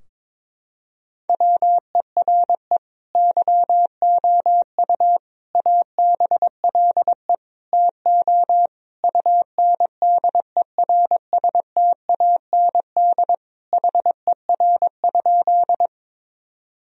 Morse code, heard as a keyed tone, about 22 wpm, 700 Hz. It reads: WERE YOU ABLE TO UNDERSTAND HER?
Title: Sentences from Top 1000 Words - Review Part 2 - 22wpm